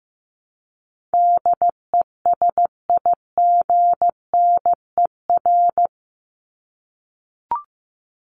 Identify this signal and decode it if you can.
Morse code, heard as a keyed tone, about 15 wpm, 700 Hz. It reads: DESIGNER
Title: Top 1501-1750 Words 15wpm